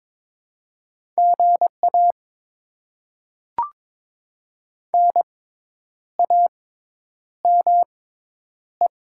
Morse code, heard as a keyed tone, about 22 wpm, spaced as for 9 wpm, 700 Hz. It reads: GA NAME
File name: Top 20 QSO Elements - 6x Character Spacing - 22wpm